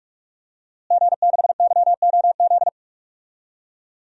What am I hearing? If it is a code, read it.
Morse code, heard as a keyed tone, about 42 wpm, 700 Hz. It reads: G6YOZ
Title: DXCC-100 Call Signs - 5 Characters - Repeated 2x - 45wpm